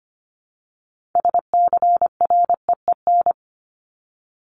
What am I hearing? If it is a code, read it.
Morse code, heard as a keyed tone, about 25 wpm, 700 Hz. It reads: SCREEN
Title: Top 751-1000 Words 25wpm